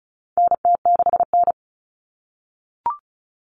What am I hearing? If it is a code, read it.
Morse code, heard as a keyed tone, about 34 wpm, 700 Hz. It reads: NT6N
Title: All Cut Numbers - 4-Digit 35wpm